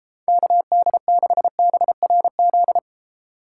Morse code, heard as a keyed tone, about 31 wpm, 700 Hz. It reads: KD6BRZ